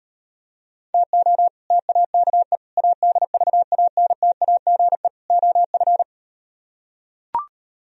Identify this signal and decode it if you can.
Morse code, heard as a keyed tone, about 38 wpm, 700 Hz. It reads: TO TAKE ADVANTAGE OF